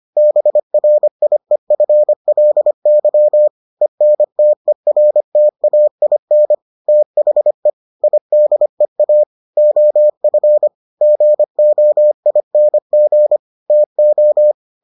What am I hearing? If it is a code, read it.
Morse code, heard as a keyed tone, about 25 wpm, 600 Hz. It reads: BRIEFLY ENTERTAIN THE IDEA OF GOING TO